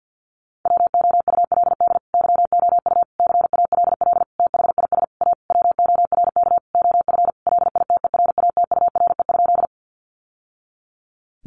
Morse code, heard as a keyed tone, about 50 wpm, 700 Hz. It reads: WOULD YOU CALL THIS A WORK OF LITERATURE?
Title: Sentences from Top 600 Words 50wpm